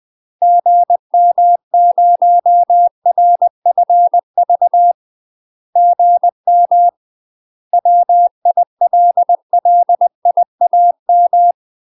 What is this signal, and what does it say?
Morse code, heard as a keyed tone, about 20 wpm, 700 Hz. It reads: GM0RFV GM WILLIAM
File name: UK MASTER QSOs TRAINING / 20WPM / UK MASTER QSOs 0053